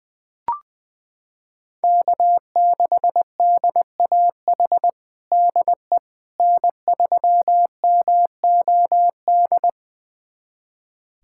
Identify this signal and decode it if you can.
Morse code, heard as a keyed tone, about 20 wpm, 700 Hz. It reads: K6DAH DE N3MOD
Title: USA Callsign DE Callsign - 5 Characters - 20wpm